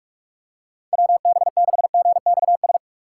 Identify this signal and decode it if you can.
Morse code, heard as a keyed tone, about 46 wpm, 700 Hz. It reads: WB6GXS
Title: USA Call Signs - 6 Characters - 6x Character Spacing - 45wpm